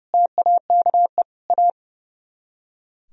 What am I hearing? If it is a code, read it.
Morse code, heard as a keyed tone, about 38 wpm, 700 Hz. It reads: TAKE A